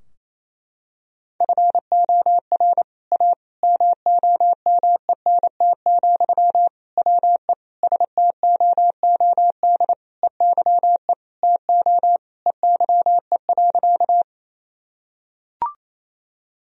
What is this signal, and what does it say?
Morse code, heard as a keyed tone, about 27 wpm, 700 Hz. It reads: FOR A MOMENT, WE STOOD EYE TO EYE.
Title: Sentences from Top 1000 Words - Review Part 2 - 28wpm